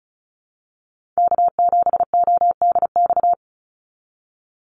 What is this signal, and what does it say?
Morse code, heard as a keyed tone, about 34 wpm, 700 Hz. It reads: K7ODX
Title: NA SOTA Chaser Callsigns 35wpm